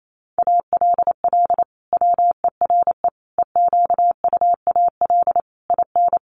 Morse code, heard as a keyed tone, about 28 wpm, 700 Hz. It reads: ALL WERE EQUAL IN
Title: Sentences from Top 1400 Words 28wpm